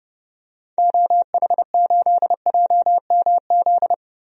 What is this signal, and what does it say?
Morse code, heard as a keyed tone, about 30 wpm, 700 Hz. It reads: OH8JMZ
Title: DXCC-100 Call Signs - 6 Characters - Repeated 2x - 30wpm